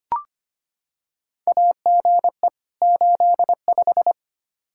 Morse code, heard as a keyed tone, about 25 wpm, 700 Hz. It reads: AGE 85
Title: Practice QSO - Exchange 3 - More Info - 25wpm